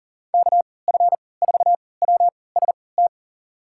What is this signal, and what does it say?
Morse code, heard as a keyed tone, about 40 wpm, spaced as for 23 wpm, 700 Hz. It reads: KF4WST
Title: USA Call Signs - 6 Characters - 3x Character Spacing - 40wpm